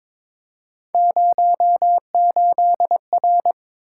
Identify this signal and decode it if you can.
Morse code, heard as a keyed tone, about 22 wpm, 700 Hz. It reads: 08R
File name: practice / 22wpm / Three Letter-Number Combinations - 22wpm